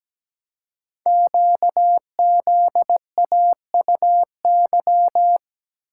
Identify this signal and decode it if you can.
Morse code, heard as a keyed tone, about 17 wpm, 700 Hz. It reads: QZAUY
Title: Code Groups - 5 Letters - 17wpm